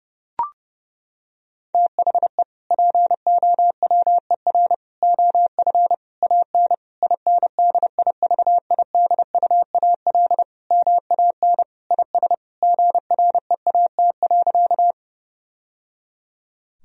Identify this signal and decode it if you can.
Morse code, heard as a keyed tone, about 32 wpm, 700 Hz. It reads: THE POWER OF AN INDIVIDUAL MAN IS GREAT.
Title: Sentences from Top 1000 Words - Review Part 4 - 30wpm